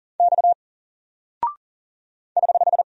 Morse code, heard as a keyed tone, about 39 wpm, 700 Hz. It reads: X <HH>
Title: Single Character (Letter:Number:Punctuation:Prosign) Alt 40wpm